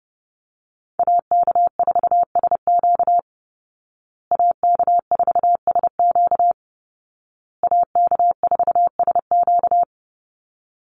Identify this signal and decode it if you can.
Morse code, heard as a keyed tone, about 30 wpm, 700 Hz. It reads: AK4SQ AK4SQ AK4SQ
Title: Call Signs Repeated 2x - 30wpm